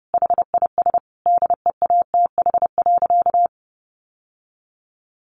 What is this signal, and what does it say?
Morse code, heard as a keyed tone, about 30 wpm, 700 Hz. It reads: HIS DEATH.
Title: Sentences from Top 1100 Words 30wpm